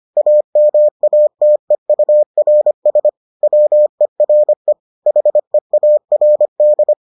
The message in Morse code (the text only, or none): AMATEURS WERE HEARD